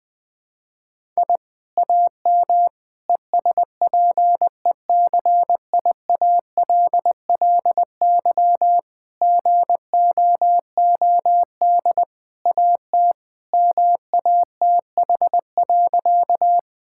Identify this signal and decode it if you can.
Morse code, heard as a keyed tone, about 21 wpm, 700 Hz. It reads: I AM ESPECIALLY GOOD AT MATH.
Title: Sentences from Top 1000 Words - Review Part 6 - 20wpm